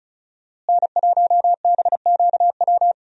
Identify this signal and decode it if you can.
Morse code, heard as a keyed tone, about 35 wpm, 700 Hz. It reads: N1BQW
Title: USA Call Signs - 5 Characters - Repeated 2x - 35wpm